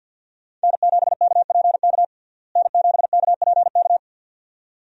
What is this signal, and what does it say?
Morse code, heard as a keyed tone, about 50 wpm, 700 Hz. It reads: N7KPK N7KPK
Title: USA Call Signs - 5 Characters - Repeated 2x - 50wpm